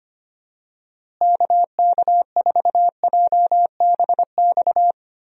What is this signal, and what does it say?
Morse code, heard as a keyed tone, about 25 wpm, 700 Hz. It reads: KK4JBX
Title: USA Call Signs - 6 Characters - Repeated 2x - 25wpm